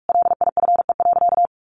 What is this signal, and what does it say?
Morse code, heard as a keyed tone, about 44 wpm, 700 Hz. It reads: LIFE.
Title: Sentences from Top 1200 Words 45wpm